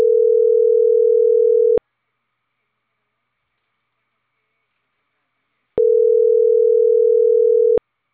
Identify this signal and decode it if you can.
Ringback tone (North America), 440 + 480 Hz.